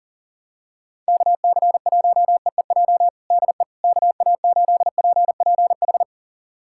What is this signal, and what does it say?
Morse code, heard as a keyed tone, about 40 wpm, 700 Hz. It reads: KC1EEJ DE KA8PPH
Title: USA Callsign DE Callsign - 6 Characters - 40wpm